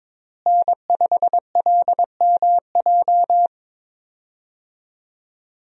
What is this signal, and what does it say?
Morse code, heard as a keyed tone, about 22 wpm, 700 Hz. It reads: N5LMJ
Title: USA Call Signs - 5 Characters - Repeated 3x - 22wpm